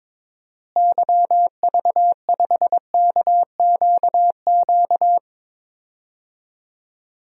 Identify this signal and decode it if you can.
Morse code, heard as a keyed tone, about 22 wpm, 700 Hz. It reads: YV5KQQ
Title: DXCC-100 Call Signs - 6 Characters - Repeated 3x - 22wpm